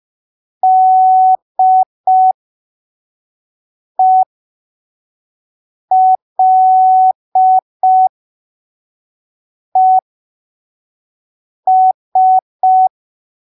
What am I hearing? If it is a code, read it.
Morse code, heard as a keyed tone, about 5 wpm, spaced as for 3 wpm, 750 Hz. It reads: DELES
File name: arquivos / texto / texto 30 05ppm